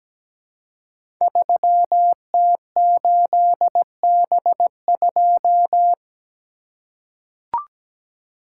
Morse code, heard as a keyed tone, about 17 wpm, 700 Hz. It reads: 3T8B2
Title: Code Groups - 5 Letter:Numbers - 17wpm